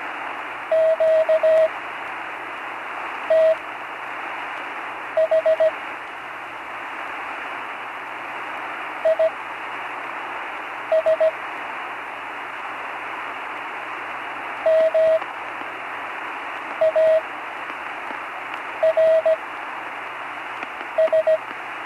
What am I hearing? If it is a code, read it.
Morse code, heard as a keyed tone, about 17 wpm, spaced as for 5 wpm, 640 Hz, with radio noise. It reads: QTH IS MARS
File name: CW Learning Programs / Learning Morse Code (from Tune in the World) / 35 answers